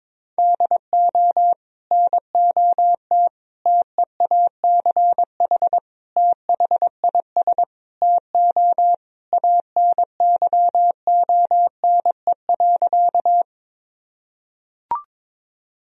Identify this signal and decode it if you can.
Morse code, heard as a keyed tone, about 22 wpm, 700 Hz. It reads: DO NOT TEACH THIS TO ANYONE.
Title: Sentences from Top 1000 Words - Review Part 2 - 22wpm